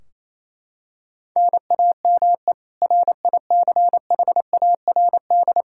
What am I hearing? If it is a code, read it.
Morse code, heard as a keyed tone, about 27 wpm, 700 Hz. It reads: NAME RICHARD
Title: Practice QSO - Exchange 1 - Beginning - 28wpm